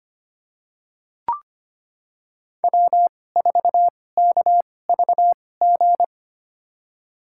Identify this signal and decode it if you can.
Morse code, heard as a keyed tone, about 25 wpm, spaced as for 18 wpm, 700 Hz. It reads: W4KVG
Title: USA Call Signs - 5 Characters - 2x Character Spacing - 25wpm